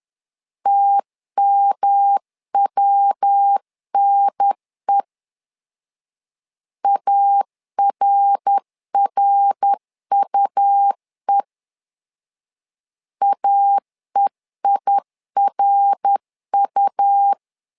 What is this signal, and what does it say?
Morse code, heard as a keyed tone, about 10 wpm, 800 Hz. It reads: TMWNE ARRUE AEIRU
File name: LEZIONI / 3 MWRO / 3 MWR 10wpm 1952